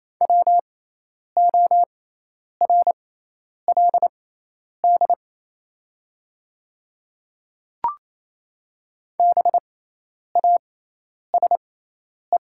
Morse code, heard as a keyed tone, about 27 wpm, spaced as for 10 wpm, 700 Hz. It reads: WORLD BASE
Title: 6x Character Spacing - Top 300 Words - 28wpm